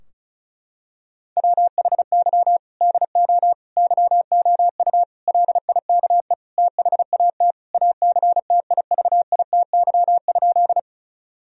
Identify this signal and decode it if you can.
Morse code, heard as a keyed tone, about 34 wpm, 700 Hz. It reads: WHY DO YOU LIKE THAT ACTIVITY?
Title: Sentences from Top 1000 Words - Review Part 4 - 35wpm